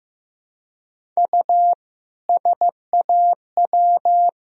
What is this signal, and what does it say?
Morse code, heard as a keyed tone, about 15 wpm, 700 Hz. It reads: U SAW
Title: Sentences from Top 300 Words and Abbreviations 15wpm